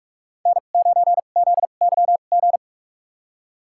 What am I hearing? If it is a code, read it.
Morse code, heard as a keyed tone, about 45 wpm, spaced as for 32 wpm, 700 Hz. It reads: N9ZYG
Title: USA Call Signs - 2x Character Spacing - 45wpm